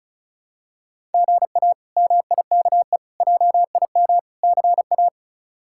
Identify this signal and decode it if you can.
Morse code, heard as a keyed tone, about 34 wpm, 700 Hz. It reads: GA MIKE JIM CA
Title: K1USN SST QSO Practice - 35wpm